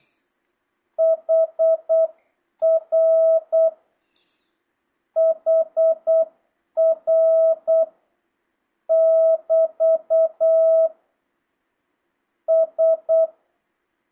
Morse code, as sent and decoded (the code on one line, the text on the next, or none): .... .-. / .... .-. / -...- / ...
HR HR <BT> S